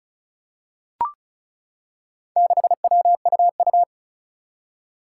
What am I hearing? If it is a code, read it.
Morse code, heard as a keyed tone, about 36 wpm, 700 Hz. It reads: 6WUU
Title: Four Challenging Letter-Number Combinations 35wpm